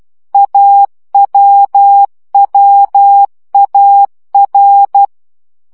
Morse code, heard as a keyed tone, about 12 wpm, 800 Hz. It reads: AWWAR
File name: CW gyakorlas a r w 10 baud